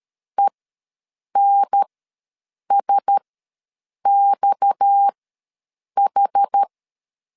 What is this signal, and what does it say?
Morse code, heard as a keyed tone, about 13 wpm, spaced as for 7 wpm, 800 Hz. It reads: ENSXH
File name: LEZIONI / 6 PKX0 / 6 PKX ws 12wpm 3106